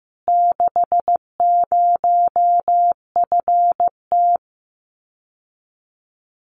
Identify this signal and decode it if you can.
Morse code, heard as a keyed tone, about 15 wpm, 700 Hz. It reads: 60FT
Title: Common QSO Phrases Encore 15wpm